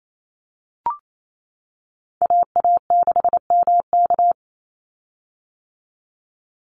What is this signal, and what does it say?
Morse code, heard as a keyed tone, about 28 wpm, 700 Hz. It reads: AA6MK
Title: Call Signs - 28wpm